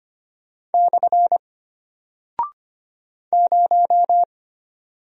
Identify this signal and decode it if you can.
Morse code, heard as a keyed tone, about 25 wpm, 700 Hz. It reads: / 0
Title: Single Character (Letter:Number:Punctuation:Prosign) Alt 25wpm